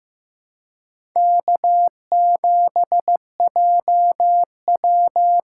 Morse code, heard as a keyed tone, about 15 wpm, 700 Hz. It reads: K7JW
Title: USA Call Signs - 4 Characters - 4x Character Spacing - 15wpm